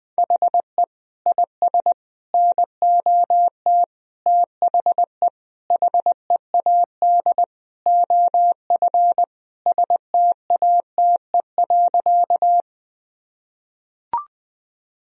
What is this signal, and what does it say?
Morse code, heard as a keyed tone, about 21 wpm, 700 Hz. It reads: HE IS NOT THE HEAD OF STATE.
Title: Sentences from Top 1000 Words - Review Part 2 - 20wpm